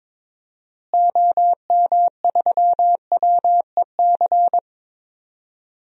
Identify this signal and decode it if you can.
Morse code, heard as a keyed tone, about 22 wpm, 700 Hz. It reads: OM3WEC
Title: DXCC-100 Call Signs - 6 Characters - Repeated 2x - 22wpm